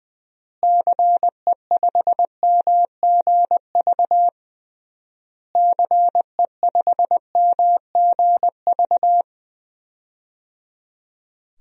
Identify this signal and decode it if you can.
Morse code, heard as a keyed tone, about 20 wpm, 700 Hz. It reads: CE5MGV CE5MGV
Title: DXCC-100 Call Signs - 6 Characters - Repeated 3x - 20wpm